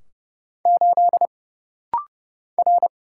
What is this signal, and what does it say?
Morse code, heard as a keyed tone, about 29 wpm, 700 Hz. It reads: 8 R
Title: Single Letter-Number Alt - Rapid-Fire 30wpm